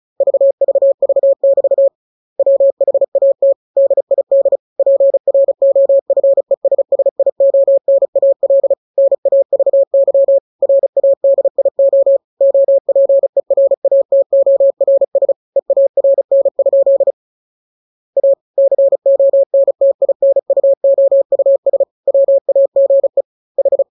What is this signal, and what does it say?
Morse code, heard as a keyed tone, about 34 wpm, 550 Hz. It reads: VVV<BT> WHAT DID PROFESSIONAL NAVY RADIO OPERATORS EARN? A CONTINUOUS WAGE H